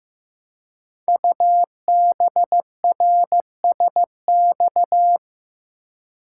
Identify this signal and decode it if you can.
Morse code, heard as a keyed tone, about 15 wpm, 700 Hz. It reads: UBRSX
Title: Code Groups - 5 Letters - 15wpm